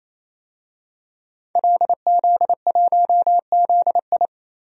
Morse code, heard as a keyed tone, about 27 wpm, 700 Hz. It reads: LZ1ZI
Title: DXCC-100 Call Signs - 5 Characters - 28wpm